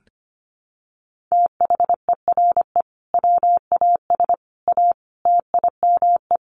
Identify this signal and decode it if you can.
Morse code, heard as a keyed tone, about 25 wpm, 700 Hz. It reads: THERE WAS A TIME
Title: Sets of 5 Words 25wpm